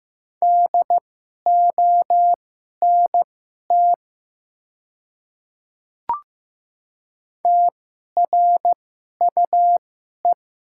Morse code, heard as a keyed tone, about 15 wpm, spaced as for 11 wpm, 690 Hz. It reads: DONT TRUE
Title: 2x Character Spacing - Top 500 Words - 15wpm